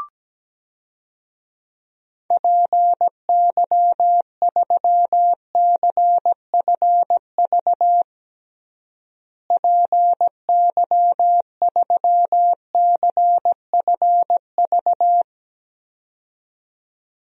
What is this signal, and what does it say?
Morse code, heard as a keyed tone, about 17 wpm, 700 Hz. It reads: PY3CFV PY3CFV
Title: DXCC-100 Call Signs - 6 Characters - Repeated 2x - 17wpm